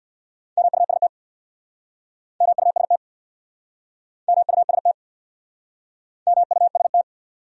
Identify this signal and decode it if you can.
Morse code, heard as a keyed tone, about 65 wpm, 700 Hz. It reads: MUST MUST MUST MUST